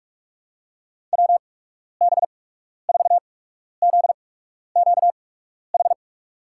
Morse code, heard as a keyed tone, about 47 wpm, spaced as for 12 wpm, 700 Hz. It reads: WB4ZQH
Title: USA Call Signs - 6 Characters - 8x Character Spacing - 45wpm